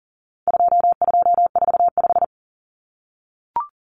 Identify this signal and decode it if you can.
Morse code, heard as a keyed tone, about 40 wpm, 700 Hz. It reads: 2245
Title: Numbers - 4-Digit 40wpm